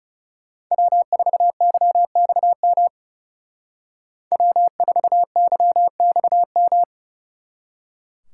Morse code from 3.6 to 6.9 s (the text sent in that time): W4YXM